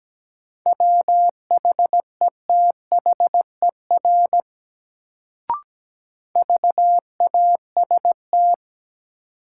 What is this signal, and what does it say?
Morse code, heard as a keyed tone, about 17 wpm, 700 Hz. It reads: WHETHER VAST
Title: Sentences from Top 1300 Words - New Word Drill Alt - 17wpm